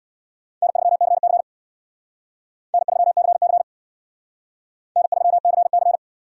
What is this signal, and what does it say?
Morse code, heard as a keyed tone, about 65 wpm, 700 Hz. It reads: N3CC N3CC N3CC